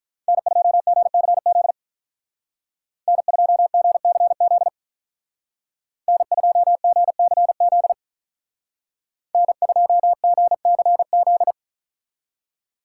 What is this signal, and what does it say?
Morse code, heard as a keyed tone, about 44 wpm, 700 Hz. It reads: N2GCZ N2GCZ N2GCZ N2GCZ